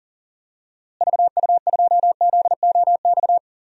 Morse code, heard as a keyed tone, about 40 wpm, 700 Hz. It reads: VU2ZOX
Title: DXCC-100 Call Signs - 6 Characters - Repeated 2x - 40wpm